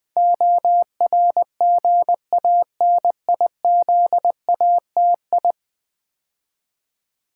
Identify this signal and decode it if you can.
Morse code, heard as a keyed tone, about 20 wpm, 700 Hz. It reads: ORGANIZATI
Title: Build A Word - 5 Letters At A Time - 12-Letter Words - 20wpm